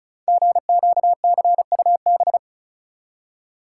Morse code, heard as a keyed tone, about 34 wpm, 700 Hz. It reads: GQCUB
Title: Code Groups - 5 Letter:Numbers - 35wpm